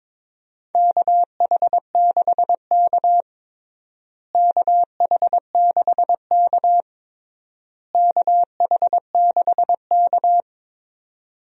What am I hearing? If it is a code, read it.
Morse code, heard as a keyed tone, about 22 wpm, 700 Hz. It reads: KH6K KH6K KH6K
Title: DXCC-100 Call Signs - 4 Characters - Repeated 3x - 22wpm